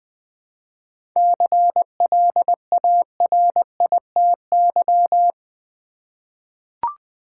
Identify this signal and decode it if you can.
Morse code, heard as a keyed tone, about 20 wpm, 700 Hz. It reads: CLARITY
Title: 7 Letter Words - First 3 Letters Given - 20wpm